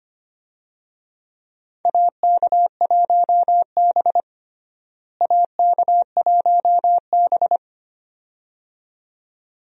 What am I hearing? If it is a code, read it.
Morse code, heard as a keyed tone, about 25 wpm, 700 Hz. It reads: AK1B AK1B
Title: USA Call Signs - 4 Characters - Repeated 2x - 25wpm